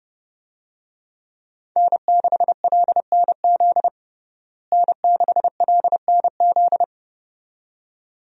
Morse code, heard as a keyed tone, about 32 wpm, 700 Hz. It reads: N6LNZ N6LNZ
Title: USA Call Signs - 5 Characters - Repeated 2x - 30wpm